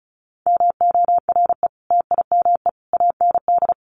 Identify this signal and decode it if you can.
Morse code, heard as a keyed tone, about 34 wpm, 700 Hz. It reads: MORE TIME AND
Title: Sentences from Top 1400 Words 35wpm